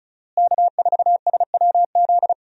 Morse code, heard as a keyed tone, about 36 wpm, 700 Hz. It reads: K4SWZ
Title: USA Call Signs - 5 Characters - Repeated 2x - 35wpm